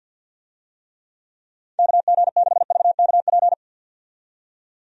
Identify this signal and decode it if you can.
Morse code, heard as a keyed tone, about 49 wpm, 700 Hz. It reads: KG6VKP
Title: USA Call Signs - 6 Characters - 50wpm